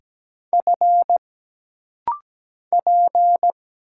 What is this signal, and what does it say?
Morse code, heard as a keyed tone, about 17 wpm, 700 Hz. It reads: F P
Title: Single Letters - Rapid-Fire 17wpm